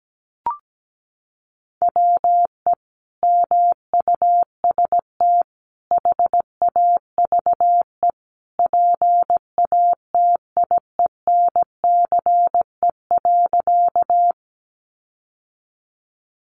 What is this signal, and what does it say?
Morse code, heard as a keyed tone, about 17 wpm, 700 Hz. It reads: WE MUST HAVE PATIENCE.
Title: Sentences from Top 1300 Words 17wpm